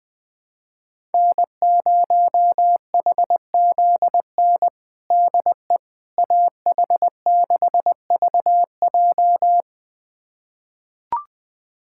Morse code, heard as a keyed tone, about 21 wpm, 700 Hz. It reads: N0HZN DE AH6VJ
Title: USA Callsign DE Callsign - 5 Characters - 20wpm